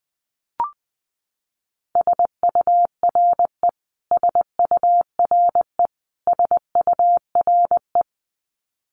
Morse code, heard as a keyed tone, about 19 wpm, 700 Hz. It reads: SURE SURE SURE
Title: Top 500 Words Repeated 20wpm